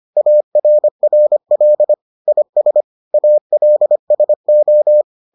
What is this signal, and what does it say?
Morse code, heard as a keyed tone, about 25 wpm, 600 Hz. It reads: ARRL IS ALSO